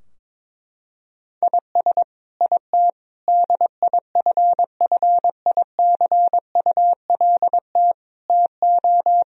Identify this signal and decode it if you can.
Morse code, heard as a keyed tone, about 22 wpm, 700 Hz. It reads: IS IT DIFFICULT TO
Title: Sentences from Top 1000 Words - Review Part 4 - 22wpm